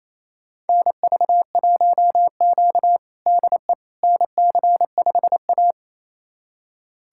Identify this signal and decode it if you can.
Morse code, heard as a keyed tone, about 28 wpm, 700 Hz. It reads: NV1Q DE NC5A
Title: USA Callsign DE Callsign - 4 Characters - 28wpm